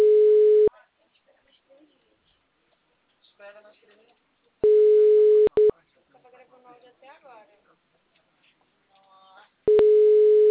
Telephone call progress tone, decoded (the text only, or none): ringback tone